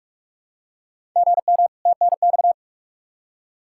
Morse code, heard as a keyed tone, about 46 wpm, 700 Hz. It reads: GM TNX
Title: Practice QSO - Exchange 1 - Beginning - 45wpm